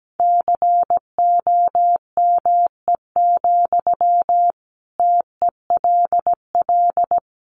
Morse code, heard as a keyed tone, about 17 wpm, 700 Hz. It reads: COME, TELL
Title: Sentences from Top 800 Words 17wpm